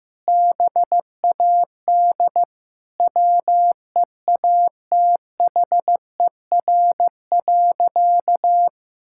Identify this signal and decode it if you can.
Morse code, heard as a keyed tone, about 16 wpm, 700 Hz. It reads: BAD WEATHER.
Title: Sentences from Top 1000 Words - Review Part 6 - 15wpm